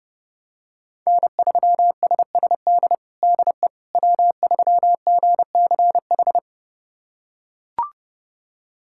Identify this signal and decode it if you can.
Morse code, heard as a keyed tone, about 30 wpm, 700 Hz. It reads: N3SSD DE W3GCH
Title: USA Callsign DE Callsign - 5 Characters - 30wpm